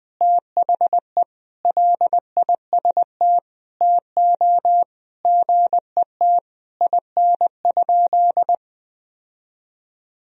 Morse code, heard as a keyed tone, about 21 wpm, 700 Hz. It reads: THE LIST TO GET IN?
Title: Sentences from Top 1000 Words - Review Part 2 - 20wpm